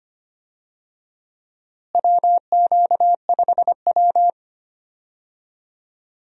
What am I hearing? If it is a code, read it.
Morse code, heard as a keyed tone, about 25 wpm, 700 Hz. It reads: WQ5W
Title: USA Call Signs - 4 Characters - Repeated 2x - 25wpm